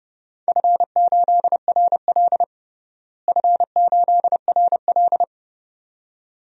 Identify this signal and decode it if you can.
Morse code, heard as a keyed tone, about 30 wpm, 700 Hz. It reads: F8RL F8RL
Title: DXCC-100 Call Signs - 4 Characters - Repeated 2x - 30wpm